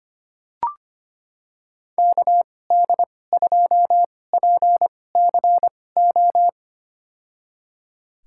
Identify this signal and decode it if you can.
Morse code, heard as a keyed tone, about 25 wpm, spaced as for 18 wpm, 700 Hz. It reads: KD2PCO
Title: USA Call Signs - 6 Characters - 2x Character Spacing - 25wpm